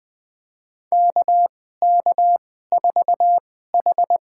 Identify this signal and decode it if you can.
Morse code, heard as a keyed tone, about 20 wpm, spaced as for 14 wpm, 700 Hz. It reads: KK4H
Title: USA Call Signs - 2x Character Spacing - 20wpm